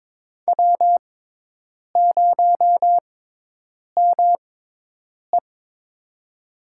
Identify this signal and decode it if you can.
Morse code, heard as a keyed tone, about 22 wpm, spaced as for 8 wpm, 700 Hz. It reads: W0ME
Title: USA Call Signs - 4 Characters - 6x Character Spacing - 22wpm